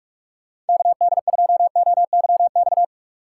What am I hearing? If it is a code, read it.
Morse code, heard as a keyed tone, about 59 wpm, spaced as for 51 wpm, 700 Hz. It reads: KD2OYX
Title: Speed Racing - USA Call Signs 45wpm